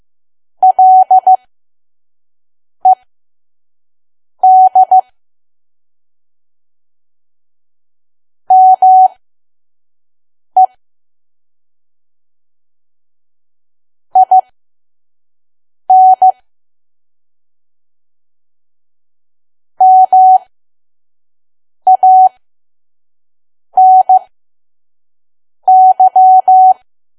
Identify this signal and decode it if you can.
Morse code, heard as a keyed tone, about 15 wpm, spaced as for 5 wpm, 750 Hz. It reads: LED ME IN MANY
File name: Archive / 5 WPM files / PM